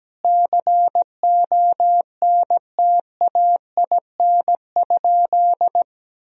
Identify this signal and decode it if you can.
Morse code, heard as a keyed tone, about 17 wpm, 700 Hz. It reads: CONTAIN?
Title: Sentences from Top 1000 Words - Review Part 2 - 17wpm